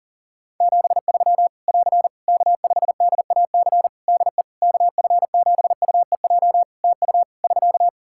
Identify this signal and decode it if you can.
Morse code, heard as a keyed tone, about 40 wpm, 700 Hz. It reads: 73 <AR> K5DAC DE KF7UEJ TU <SK>
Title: Practice QSO - Exchange 4 - Ending - 40wpm